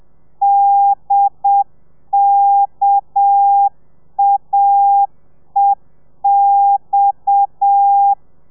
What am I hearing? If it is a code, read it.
Morse code, heard as a keyed tone, about 7 wpm, 800 Hz. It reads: DKAEX
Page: Lezione 1: audio a 30 CPM